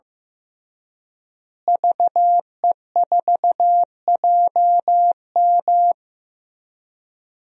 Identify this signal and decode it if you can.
Morse code, heard as a keyed tone, about 16 wpm, 700 Hz. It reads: VE4JM
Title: DXCC-100 Call Signs - 5 Characters - Repeated 3x - 15wpm